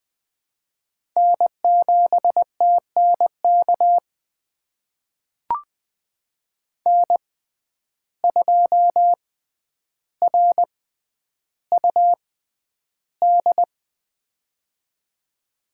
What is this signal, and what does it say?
Morse code, heard as a keyed tone, about 21 wpm, spaced as for 10 wpm, 700 Hz. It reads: N7TNK N2RUD
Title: USA Call Signs - 5 Characters - 6x Character Spacing - 20wpm